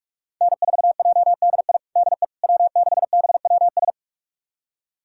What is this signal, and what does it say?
Morse code, heard as a keyed tone, about 46 wpm, 700 Hz. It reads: N4JDI DE W6BWS
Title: USA Callsign DE Callsign - 5 Characters - 45wpm